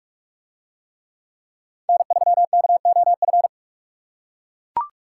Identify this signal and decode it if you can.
Morse code, heard as a keyed tone, about 43 wpm, 700 Hz. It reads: N3KOF